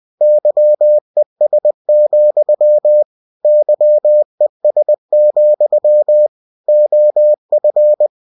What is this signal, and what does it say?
Morse code, heard as a keyed tone, about 20 wpm, 600 Hz. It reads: YES, YES, OF
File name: cw-books / winnie-the-pooh / winnie-20wpm-page-0084